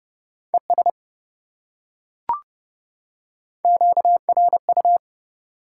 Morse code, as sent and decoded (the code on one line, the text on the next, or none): . ... / --.- .-. ..-
ES QRU